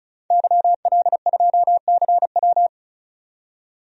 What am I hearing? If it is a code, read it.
Morse code, heard as a keyed tone, about 34 wpm, 700 Hz. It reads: YL2CW
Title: DXCC-100 Call Signs - 5 Characters - 35wpm